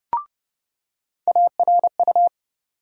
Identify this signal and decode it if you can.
Morse code, heard as a keyed tone, about 30 wpm, 700 Hz. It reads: ARU
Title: USA Call Sign Postfixes - 3 Letters 30wpm